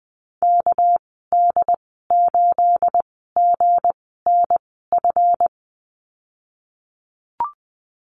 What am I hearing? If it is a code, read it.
Morse code, heard as a keyed tone, about 20 wpm, spaced as for 15 wpm, 700 Hz. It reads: KD8GNF
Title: 245 Lesson 39g - Letter z - Call Signs 20wpm